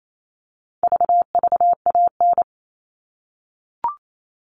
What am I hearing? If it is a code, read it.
Morse code, heard as a keyed tone, about 28 wpm, 700 Hz. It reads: VVAN
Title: All Cut Numbers - 4-Digit 28wpm